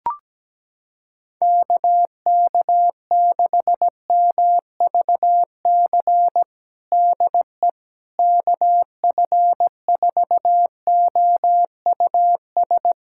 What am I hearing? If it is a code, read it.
Morse code, heard as a keyed tone, about 17 wpm, 700 Hz. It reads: KK6MVC DE KF4OUS
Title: USA Callsign DE Callsign - 6 Characters - 17wpm